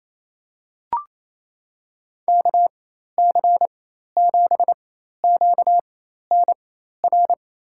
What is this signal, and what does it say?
Morse code, heard as a keyed tone, about 28 wpm, spaced as for 13 wpm, 700 Hz. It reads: KC7QNR
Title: USA Call Signs - 6 Characters - 4x Character Spacing - 28wpm